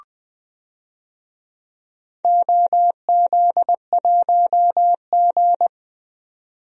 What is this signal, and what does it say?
Morse code, heard as a keyed tone, about 20 wpm, 700 Hz. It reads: OZ1G
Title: DXCC-100 Call Signs - 4 Characters - 20wpm